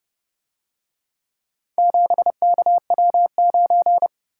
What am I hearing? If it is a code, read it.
Morse code, heard as a keyed tone, about 31 wpm, 700 Hz. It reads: 7KW9